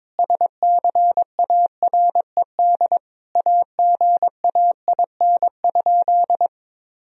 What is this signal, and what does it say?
Morse code, heard as a keyed tone, about 22 wpm, 700 Hz. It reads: SCARED AGAIN?
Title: Sentences from Top 1000 Words - Review Part 4 - 22wpm